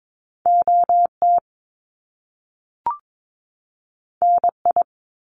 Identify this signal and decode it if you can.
Morse code, heard as a keyed tone, about 22 wpm, 700 Hz. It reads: OT NI
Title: Top Two Letter Combinations in Words 22wpm